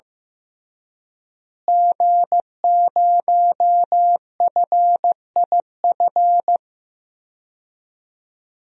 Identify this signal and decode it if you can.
Morse code, heard as a keyed tone, about 15 wpm, 700 Hz. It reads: G0FIF
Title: DXCC-100 Call Signs - 5 Characters - Repeated 3x - 15wpm